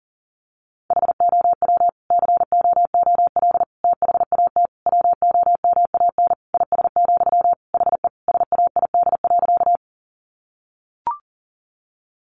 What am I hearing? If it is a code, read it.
Morse code, heard as a keyed tone, about 40 wpm, 700 Hz. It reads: HOW COOL THAT WOMAN IS, HE SAID.
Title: Sentences from Top 1200 Words 40wpm